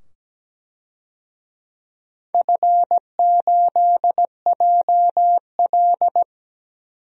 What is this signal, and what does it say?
Morse code, heard as a keyed tone, about 17 wpm, 700 Hz. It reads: F8JL
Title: DXCC-100 Call Signs - 4 Characters - 17wpm